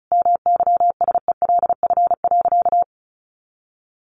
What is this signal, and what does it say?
Morse code, heard as a keyed tone, about 34 wpm, 700 Hz. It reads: MYSELF.
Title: Sentences from Top 1300 Words 35wpm